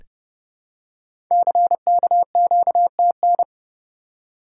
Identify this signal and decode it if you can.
Morse code, heard as a keyed tone, about 30 wpm, 700 Hz. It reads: CKQTN
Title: Code Groups - 5 Letters - 30wpm